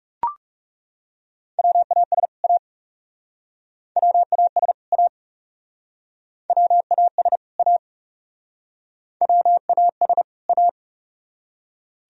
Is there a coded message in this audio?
Yes: Morse code from 1.6 to 10.7 s, heard as a keyed tone, about 42 wpm, 700 Hz. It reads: WAS A WAS A WAS A WAS A